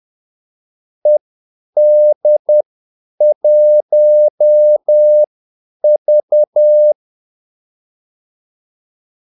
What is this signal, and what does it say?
Morse code, heard as a keyed tone, about 10 wpm, spaced as for 8 wpm, 600 Hz. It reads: ED1V
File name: Callsigns-10-8wpm